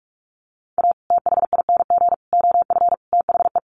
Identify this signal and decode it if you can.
Morse code, heard as a keyed tone, about 44 wpm, 700 Hz. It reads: A THING OF THE
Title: Sets of 5 Words Encore 45wpm